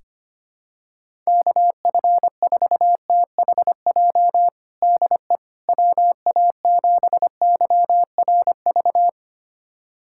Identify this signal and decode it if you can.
Morse code, heard as a keyed tone, about 25 wpm, 700 Hz. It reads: KF4THJ DE WA7YRV
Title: USA Callsign DE Callsign 25wpm